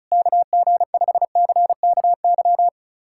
Morse code, heard as a keyed tone, about 34 wpm, 700 Hz. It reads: KG5CKY
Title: USA Callsign DE Callsign - 6 Characters - 35wpm